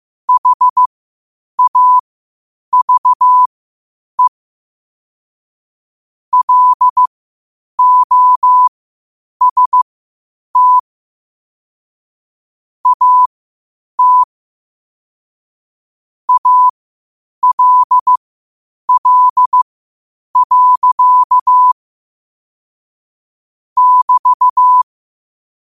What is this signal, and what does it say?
Morse code, heard as a keyed tone, about 15 wpm, spaced as for 8 wpm, 1000 Hz. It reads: HAVE LOST AT ALL. <BT>
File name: qotdfiles / 8wpm